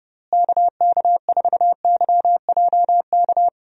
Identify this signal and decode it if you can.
Morse code, heard as a keyed tone, about 30 wpm, 700 Hz. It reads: KK4YJK
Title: USA Call Signs - 6 Characters - Repeated 2x - 30wpm